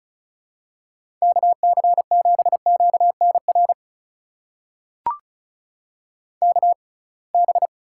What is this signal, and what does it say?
Morse code, heard as a keyed tone, about 36 wpm, spaced as for 27 wpm, 700 Hz. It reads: KC7QNR KB
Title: USA Call Signs - 6 Characters - 6x Character Spacing - 35wpm